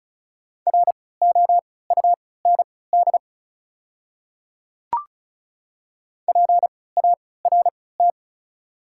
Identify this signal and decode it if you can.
Morse code, heard as a keyed tone, about 36 wpm, spaced as for 20 wpm, 700 Hz. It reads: ROUND PART
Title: 3x Character Spacing - Top 200 Words - 35wpm